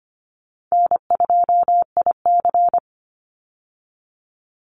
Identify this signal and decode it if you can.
Morse code, heard as a keyed tone, about 25 wpm, 700 Hz. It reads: N2IC
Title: Easy Call Signs Repeated 2x - 25wpm